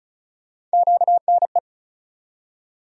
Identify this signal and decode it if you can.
Morse code, heard as a keyed tone, about 34 wpm, 700 Hz. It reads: QNE
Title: USA Call Sign Postfixes - 3 Letters 35wpm